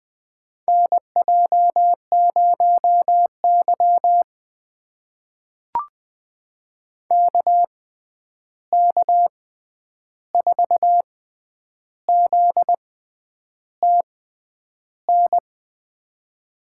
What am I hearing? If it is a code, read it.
Morse code, heard as a keyed tone, about 20 wpm, spaced as for 9 wpm, 700 Hz. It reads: NJ0Y KK4ZTN
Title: USA Call Signs - 6x Character Spacing - 20wpm